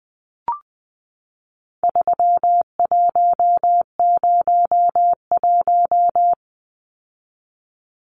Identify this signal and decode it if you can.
Morse code, heard as a keyed tone, about 20 wpm, 700 Hz. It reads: 3101
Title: Numbers - 4-Digit 20wpm